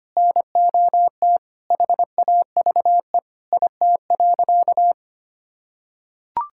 Morse code, heard as a keyed tone, about 25 wpm, 700 Hz. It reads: NOT HAVE IT.
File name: practice / 25wpm / Sentences with 3-Word Phrases - 25wpm